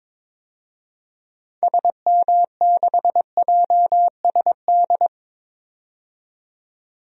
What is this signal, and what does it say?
Morse code, heard as a keyed tone, about 22 wpm, 700 Hz. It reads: SM6JSD
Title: DXCC-100 Call Signs - 6 Characters - 22wpm